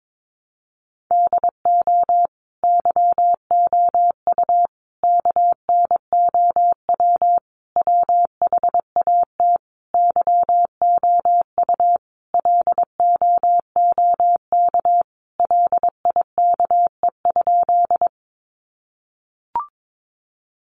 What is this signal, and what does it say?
Morse code, heard as a keyed tone, about 22 wpm, 700 Hz. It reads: DO YOU KNOW WHAT YOU LOOK LIKE?
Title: Sentences from Top 100 Words 22wpm